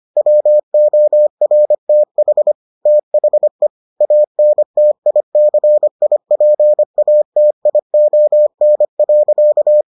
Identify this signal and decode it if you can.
Morse code, heard as a keyed tone, about 25 wpm, 600 Hz. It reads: WORTH THE ANTICIPATION.